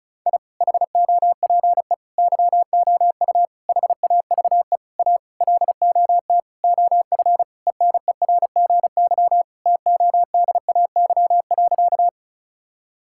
Morse code, heard as a keyed tone, about 36 wpm, 700 Hz. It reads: I HOPE YOU HAVE A LOT OF ENERGY TODAY.
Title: Sentences from Top 1000 Words - Review Part 6 - 35wpm